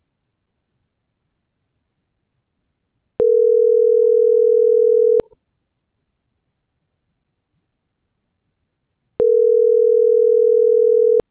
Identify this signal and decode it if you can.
Ringback tone (North America), 440 + 480 Hz.
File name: CENTRAL MABE / out maria